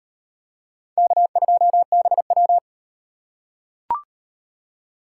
Morse code, heard as a keyed tone, about 38 wpm, 700 Hz. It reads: K2BW